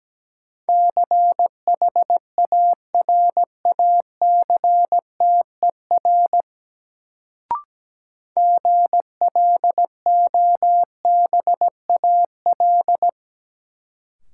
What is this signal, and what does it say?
Morse code, heard as a keyed tone, about 17 wpm, 700 Hz. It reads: CHARACTER GLOBAL
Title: Sentences from Top 1200 Words - New Word Drill Alt - 17wpm